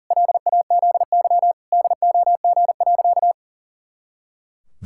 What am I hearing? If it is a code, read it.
Morse code, heard as a keyed tone, about 37 wpm, 700 Hz. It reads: LAZY DOG.
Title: Pangram Challenge v2 - 40wpm